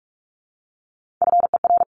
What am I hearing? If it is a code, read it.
Morse code, heard as a keyed tone, about 44 wpm, 700 Hz. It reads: FER
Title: Abbreviations Repeated 3x - 45wpm